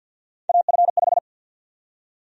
Morse code, heard as a keyed tone, about 50 wpm, 700 Hz. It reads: AF5
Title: USA Call Sign Prefixes - Repeated 2x - 50wpm